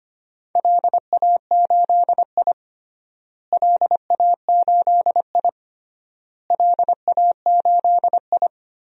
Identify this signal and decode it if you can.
Morse code, heard as a keyed tone, about 25 wpm, 700 Hz. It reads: LA8I LA8I LA8I
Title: DXCC-100 Call Signs - 4 Characters - Repeated 3x - 25wpm